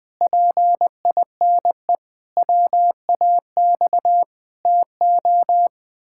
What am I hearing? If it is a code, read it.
Morse code, heard as a keyed tone, about 20 wpm, 700 Hz. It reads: PINE WAX TO
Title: Pangram Challenge v2 - 20wpm